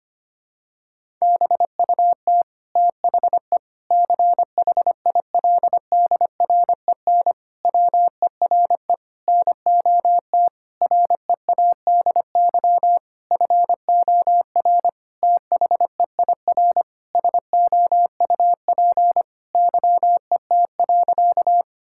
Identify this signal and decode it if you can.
Morse code, heard as a keyed tone, about 25 wpm, 700 Hz. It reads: BUT THE CHILDREN WERE NOT READY FOR THEIR SOUP YET.
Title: Sentences from Top 1000 Words - Review Part 6 - 25wpm